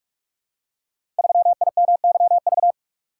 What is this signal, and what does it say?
Morse code, heard as a keyed tone, about 44 wpm, 690 Hz. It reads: 3IMYV
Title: Code Groups - 5 Letter:Numbers - 45wpm